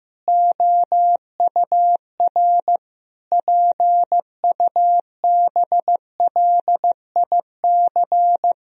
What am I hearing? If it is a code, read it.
Morse code, heard as a keyed tone, about 15 wpm, 700 Hz. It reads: OUR PUBLIC
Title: Sentences from Top 1000 Words - Review Part 3 - 15wpm